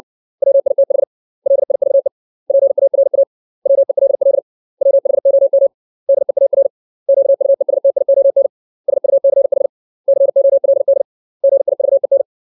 Morse code, heard as a keyed tone, about 60 wpm, 550 Hz. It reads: WITH LIVE PARA WELL WHOM DEAR QUESTION SUCH COLD GIVEN
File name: 632 words / 632 Words 60WPM